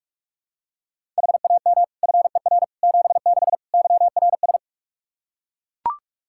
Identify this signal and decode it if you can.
Morse code, heard as a keyed tone, about 42 wpm, 700 Hz. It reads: HAM FER 76 YRS
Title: Practice QSO - Exchange 3 - More Info - 45wpm